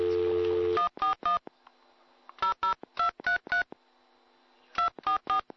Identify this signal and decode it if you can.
DTMF, heard as telephone touch tones, tones of about 110 ms. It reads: 141 00233 244